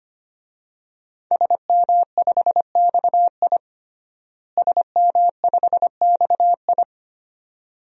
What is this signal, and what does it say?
Morse code, heard as a keyed tone, about 25 wpm, 700 Hz. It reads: SM5XI SM5XI
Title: DXCC-100 Call Signs - 5 Characters - Repeated 2x - 25wpm